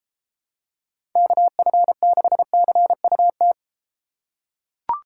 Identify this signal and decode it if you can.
Morse code, heard as a keyed tone, about 32 wpm, 700 Hz. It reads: KF6CUT